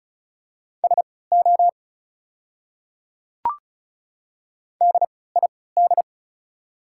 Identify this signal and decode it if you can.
Morse code, heard as a keyed tone, about 34 wpm, spaced as for 20 wpm, 700 Hz. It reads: SO DID
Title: 3x Character Spacing - Top 300 Words - 35wpm